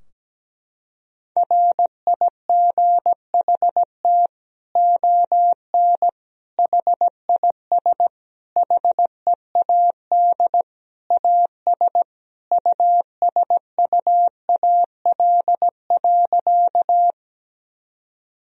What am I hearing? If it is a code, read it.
Morse code, heard as a keyed tone, about 17 wpm, 700 Hz. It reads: RIGHT ON HIS HEAD AS USUAL.
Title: Sentences from Top 1000 Words - Review Part 2 - 17wpm